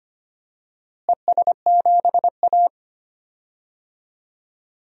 Morse code, heard as a keyed tone, about 25 wpm, 700 Hz. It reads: ES7A
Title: DXCC-100 Call Signs - 4 Characters - Repeated 3x - 25wpm